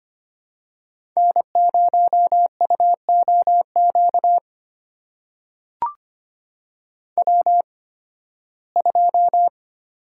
Morse code, heard as a keyed tone, about 25 wpm, spaced as for 16 wpm, 700 Hz. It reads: N0UOQ W2
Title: USA Call Signs - 5 Characters - 8x Character Spacing - 25wpm